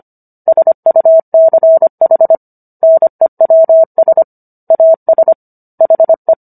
Morse code, heard as a keyed tone, about 25 wpm, 650 Hz. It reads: SUCH NEWS AS HE